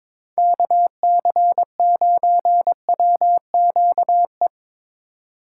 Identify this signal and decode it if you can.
Morse code, heard as a keyed tone, about 24 wpm, 700 Hz. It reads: KC9WQE
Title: Speed Racing - USA 2x3 Call Signs 22wpm